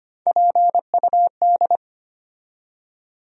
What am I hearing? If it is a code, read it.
Morse code, heard as a keyed tone, about 25 wpm, 700 Hz. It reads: PUD
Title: USA Call Sign Postfixes - 3 Letters 25wpm